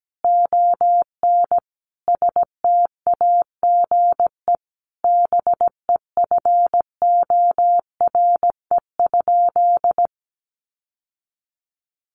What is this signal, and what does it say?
Morse code, heard as a keyed tone, about 17 wpm, 700 Hz. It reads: ON STAGE BEFORE?
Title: Sentences from Top 1200 Words 17wpm